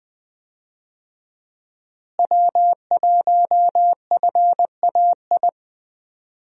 Morse code, heard as a keyed tone, about 20 wpm, 700 Hz. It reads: W1FAI
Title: USA Call Signs - 5 Characters - Repeated 2x - 20wpm